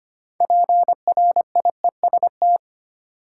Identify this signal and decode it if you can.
Morse code, heard as a keyed tone, about 25 wpm, 700 Hz. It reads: PRIEST
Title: Six Letter Words - 25wpm